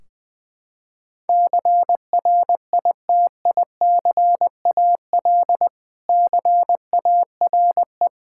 Morse code, heard as a keyed tone, about 20 wpm, 700 Hz. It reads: CRITICAL CARE
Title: Practice QSO - Exchange 3 - More Info - 20wpm